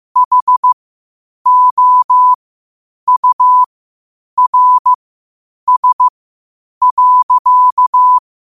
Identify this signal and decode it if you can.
Morse code, heard as a keyed tone, about 15 wpm, spaced as for 8 wpm, 1000 Hz. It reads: HOURS.
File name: qotdfiles / 8wpm